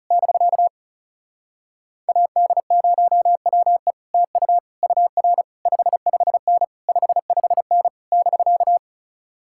Morse code, heard as a keyed tone, about 40 wpm, 700 Hz. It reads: <BK> AD0WE TU UR 55N 55N <BK>